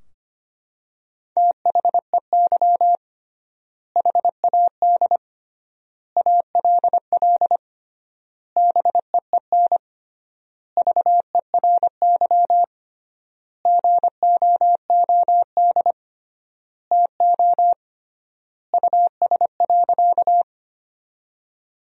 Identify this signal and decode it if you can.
Morse code, heard as a keyed tone, about 25 wpm, 700 Hz. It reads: THEY HAD ALL BEEN VERY GOOD TO US.
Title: 3x Word-Spacing - Sentences from Top 200 Words - 25wpm